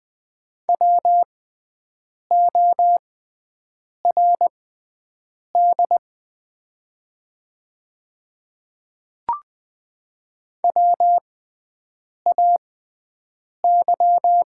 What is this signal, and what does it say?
Morse code, heard as a keyed tone, about 20 wpm, spaced as for 7 wpm, 700 Hz. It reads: WORD WAY
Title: 6x Character Spacing - Top 200 Words - 20wpm